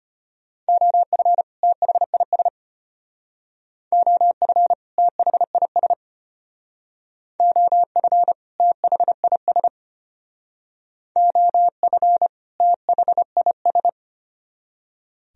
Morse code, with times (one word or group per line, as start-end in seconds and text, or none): 0.7-1.4 OF
1.6-2.5 THIS
3.9-4.7 OF
5.0-5.9 THIS
7.4-8.3 OF
8.6-9.7 THIS
11.2-12.3 OF
12.6-13.9 THIS